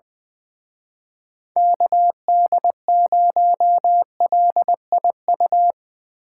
Morse code, heard as a keyed tone, about 22 wpm, 700 Hz. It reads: KD0LIU
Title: Speed Racing - USA 2x3 Call Signs 20wpm